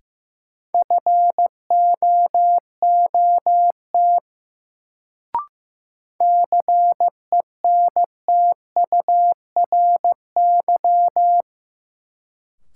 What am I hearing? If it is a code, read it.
Morse code, heard as a keyed tone, about 15 wpm, 700 Hz. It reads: FOOT CENTURY
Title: Sets of 5 Words - New Word Drill - 15wpm